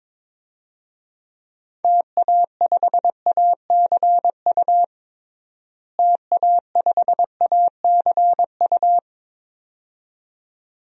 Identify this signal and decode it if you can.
Morse code, heard as a keyed tone, about 22 wpm, 700 Hz. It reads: TA5ACU TA5ACU
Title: DXCC-100 Call Signs - 6 Characters - Repeated 2x - 22wpm